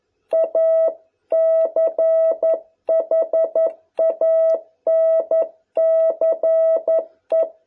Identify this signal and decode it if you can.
Morse code, heard as a keyed tone, about 11 wpm, 660 Hz. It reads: A CHANCE